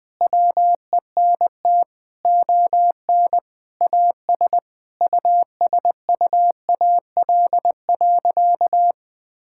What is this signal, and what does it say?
Morse code, heard as a keyed tone, about 20 wpm, 700 Hz. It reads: WENT ON AS USUAL.
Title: Sentences with 5-Word Phrases - 20wpm